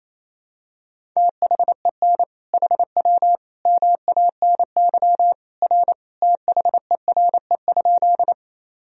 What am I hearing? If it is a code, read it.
Morse code, heard as a keyed tone, about 28 wpm, 700 Hz. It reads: THEN HW MANY R THERE?
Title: Sentences from Top 100 Words and Abbreviations 28wpm